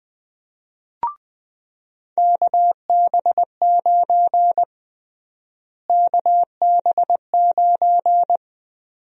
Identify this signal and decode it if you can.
Morse code, heard as a keyed tone, about 20 wpm, 700 Hz. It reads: KB9 KB9
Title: USA Call Sign Prefixes - Repeated 2x - 20wpm